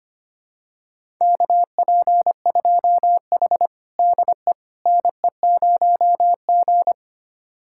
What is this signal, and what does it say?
Morse code, heard as a keyed tone, about 25 wpm, 700 Hz. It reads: KP2H DE NE0G
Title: USA Callsign DE Callsign - 4 Characters - 25wpm